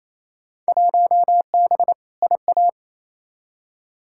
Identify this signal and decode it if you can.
Morse code, heard as a keyed tone, about 28 wpm, 700 Hz. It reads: 1B IA
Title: Field Day Exchanges Encore - 28wpm